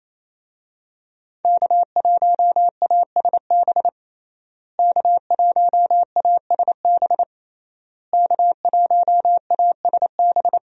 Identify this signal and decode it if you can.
Morse code, heard as a keyed tone, about 28 wpm, 700 Hz. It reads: K1ASB K1ASB K1ASB
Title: USA Call Signs - 5 Characters - Repeated 3x - 28wpm